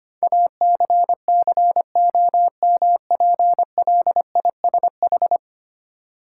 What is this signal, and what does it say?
Morse code, heard as a keyed tone, about 25 wpm, 700 Hz. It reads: ACCOMPLISH
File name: Ten Letter Words - 25wpm